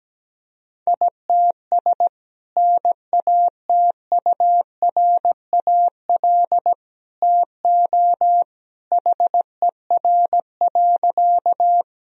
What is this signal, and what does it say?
Morse code, heard as a keyed tone, about 17 wpm, 700 Hz. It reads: ITS NATURAL TO HER.
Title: Sentences from Top 1000 Words - Review Part 4 - 17wpm